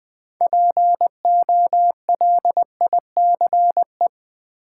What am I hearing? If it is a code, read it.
Morse code, heard as a keyed tone, about 20 wpm, 700 Hz. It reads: POLICE
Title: Six Letter Words - 20wpm